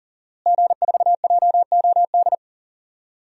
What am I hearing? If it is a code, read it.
Morse code, heard as a keyed tone, about 40 wpm, 700 Hz. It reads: G4JOD
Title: DXCC-100 Call Signs - 5 Characters - Repeated 3x - 40wpm